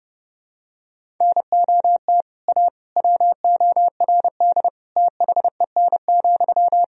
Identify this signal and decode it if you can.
Morse code, heard as a keyed tone, about 30 wpm, 700 Hz. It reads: NOT A WORD THEN,
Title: Sentences from Top 1000 Words - Review Part 1 - 30wpm